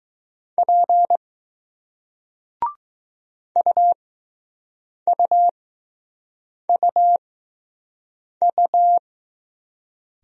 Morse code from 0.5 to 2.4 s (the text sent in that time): P